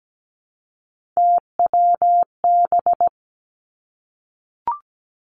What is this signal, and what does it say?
Morse code, heard as a keyed tone, about 17 wpm, 700 Hz. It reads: TWB
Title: Three Challenging Letter-Number Combinations 17wpm